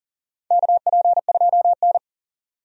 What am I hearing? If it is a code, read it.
Morse code, heard as a keyed tone, about 40 wpm, 700 Hz. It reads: KP2N
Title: DXCC-100 Call Signs - 4 Characters - 40wpm